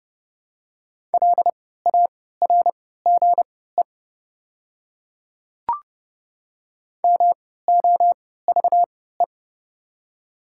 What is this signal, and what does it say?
Morse code, heard as a keyed tone, about 30 wpm, spaced as for 17 wpm, 700 Hz. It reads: LARGE MOVE
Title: 3x Character Spacing - Top 300 Words - 30wpm